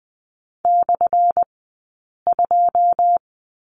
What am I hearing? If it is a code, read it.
Morse code, heard as a keyed tone, about 20 wpm, spaced as for 8 wpm, 700 Hz. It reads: /2
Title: 183 Lesson 30d - Letter b - Two Characters 20wpm